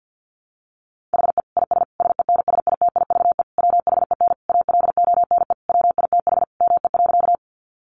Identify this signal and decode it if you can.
Morse code, heard as a keyed tone, about 50 wpm, 700 Hz. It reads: HE IS SENSITIVE WHEN ALONE WITH ME.
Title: Sentences from Top 1400 Words 50wpm